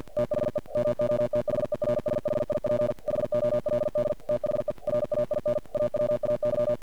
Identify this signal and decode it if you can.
Morse code, heard as a keyed tone, about 45 wpm, 590 Hz, with noise. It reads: THE MOTHERSHIP HOLD THE RAIN AWAY
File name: archived / CRSnet